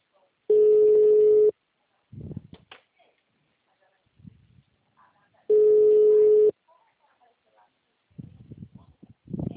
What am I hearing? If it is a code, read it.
Ringback tone (Europe, 425 Hz), with speech.